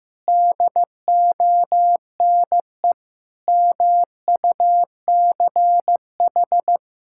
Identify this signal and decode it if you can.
Morse code, heard as a keyed tone, about 15 wpm, 700 Hz. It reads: DONE MUCH
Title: Sentences from Top 1600 Words 15wpm